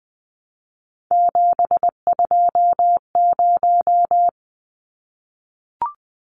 Morse code, heard as a keyed tone, about 20 wpm, 700 Hz. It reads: 720
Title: Numbers - 3-Digit 20wpm